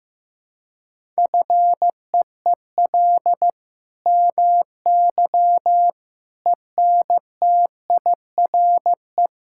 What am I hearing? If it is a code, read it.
Morse code, heard as a keyed tone, about 15 wpm, 700 Hz. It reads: FEEL MY ENTIRE
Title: Sentences from Top 1000 Words - Review Part 5 - 15wpm